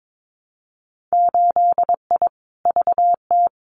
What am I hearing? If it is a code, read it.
Morse code, heard as a keyed tone, about 22 wpm, 700 Hz. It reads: 8I VT
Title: Winter Field Day Exchanges Encore 22wpm